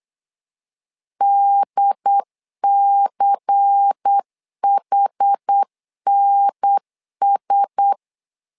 Morse code, heard as a keyed tone, about 8 wpm, 800 Hz. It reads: DCHNS
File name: LEZIONI / 5 LBVC / 5 LBVC 8wpm 2935